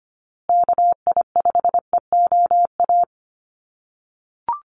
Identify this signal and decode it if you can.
Morse code, heard as a keyed tone, about 25 wpm, 700 Hz. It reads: KI5EOA
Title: USA Call Signs Repeated 3x 25wpm